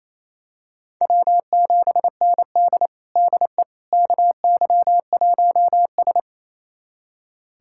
Morse code, heard as a keyed tone, about 27 wpm, 700 Hz. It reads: W7ND DE KY1S
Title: USA Callsign DE Callsign - 4 Characters - 28wpm